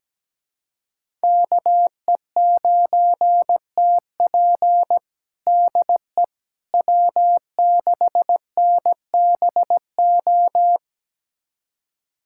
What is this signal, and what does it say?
Morse code, heard as a keyed tone, about 17 wpm, 700 Hz. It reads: KE9TP DE W6NBO
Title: USA Callsign DE Callsign - 5 Characters - 17wpm